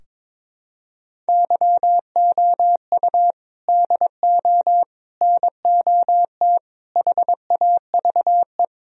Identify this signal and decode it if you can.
Morse code, heard as a keyed tone, about 22 wpm, 700 Hz. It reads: YOU DO NOT HAVE
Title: Sentences from Top 1000 Words - Review Part 4 - 22wpm